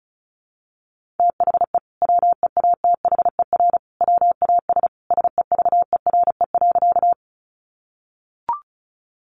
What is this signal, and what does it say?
Morse code, heard as a keyed tone, about 34 wpm, 700 Hz. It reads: THE WEATHER WAS SEVERE.
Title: Sentences from Top 1400 Words 35wpm